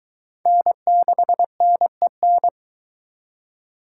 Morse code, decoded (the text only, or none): N6NEN